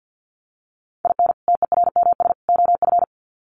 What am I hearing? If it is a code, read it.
Morse code, heard as a keyed tone, about 48 wpm, 700 Hz. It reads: IN TERMS OF
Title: Sets of 3 Words 50wpm